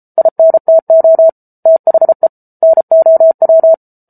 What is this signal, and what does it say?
Morse code, heard as a keyed tone, about 34 wpm, 650 Hz. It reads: INTO THE NOW